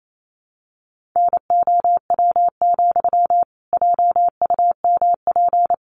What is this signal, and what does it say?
Morse code, heard as a keyed tone, about 28 wpm, 700 Hz. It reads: NOW, JUMP
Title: Sentences from Top 1300 Words 28wpm